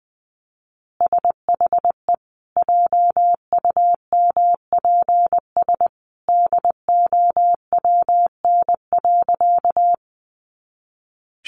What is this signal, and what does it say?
Morse code, heard as a keyed tone, about 19 wpm, 700 Hz. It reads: SHE JUMPS DOWN.
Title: Easy Phrases 20wpm